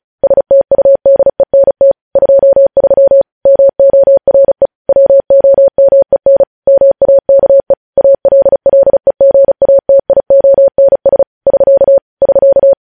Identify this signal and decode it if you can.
Morse code, heard as a keyed tone, about 36 wpm, 550 Hz. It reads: STUDENT 23 MORE WOMEN MAKE ALLEGATIONS <SK> <SK>
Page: News Headlines in Morse Code at 35 WPM